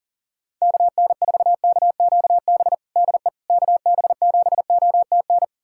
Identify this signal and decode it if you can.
Morse code, heard as a keyed tone, about 37 wpm, 700 Hz. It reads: KN4KQB DE KB7OTN
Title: USA Callsign DE Callsign 40wpm